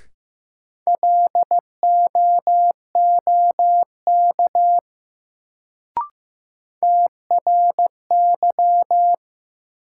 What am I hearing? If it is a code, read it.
Morse code, heard as a keyed tone, about 15 wpm, 700 Hz. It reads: LOOK TRY
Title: Sets of 2 Words - New Word Drill - 15wpm